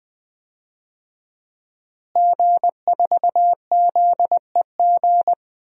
Morse code, heard as a keyed tone, about 20 wpm, 700 Hz. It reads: G4ZEG
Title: DXCC-100 Call Signs - 5 Characters - Repeated 3x - 20wpm